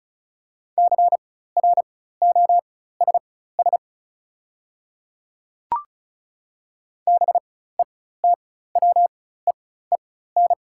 Morse code, heard as a keyed tone, about 36 wpm, spaced as for 16 wpm, 700 Hz. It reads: CROSS BETWEEN
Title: 4x Character Spacing - Top 300 Words - 35wpm